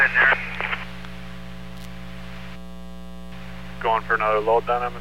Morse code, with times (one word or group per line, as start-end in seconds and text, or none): none